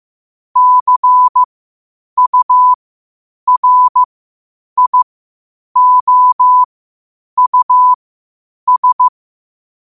Morse code, heard as a keyed tone, about 15 wpm, spaced as for 8 wpm, 1000 Hz. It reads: CURIOUS
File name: qotdfiles / 8wpm